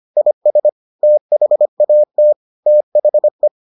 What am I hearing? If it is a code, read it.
Morse code, heard as a keyed tone, about 25 wpm, 600 Hz. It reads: IS THAT THE